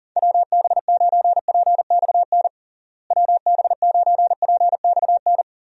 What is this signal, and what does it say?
Morse code, heard as a keyed tone, about 40 wpm, 700 Hz. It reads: WB9PXN WB9PXN
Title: USA Call Signs - 6 Characters - Repeated 3x - 40wpm